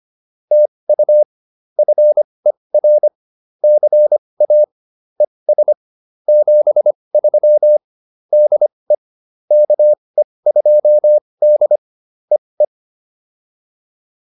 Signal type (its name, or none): Morse code